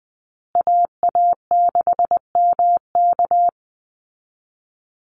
Morse code, heard as a keyed tone, about 19 wpm, 700 Hz. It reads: AA6MK
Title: Call Signs Repeated 3x - 20wpm